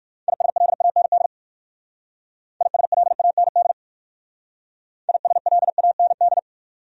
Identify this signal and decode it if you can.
Morse code, heard as a keyed tone, about 58 wpm, 690 Hz. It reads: ISLAND ISLAND ISLAND